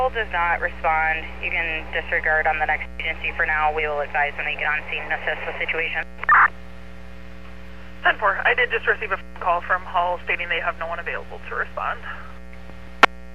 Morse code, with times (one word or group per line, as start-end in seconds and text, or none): none